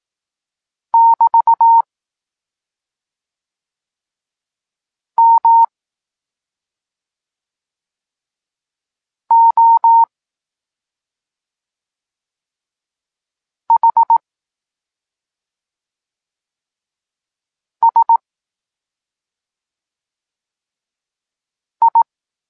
Morse code, heard as a keyed tone, about 18 wpm, spaced as for 2 wpm, 920 Hz. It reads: <BT> MOHSI
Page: Tekst 8 – doskonalenie odbioru poznanych liter